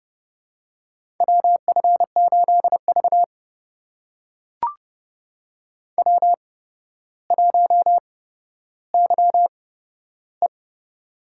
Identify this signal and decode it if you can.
Morse code, heard as a keyed tone, about 30 wpm, spaced as for 13 wpm, 700 Hz. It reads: WF8V W1YE
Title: USA Call Signs - 4 Characters - 8x Character Spacing - 30wpm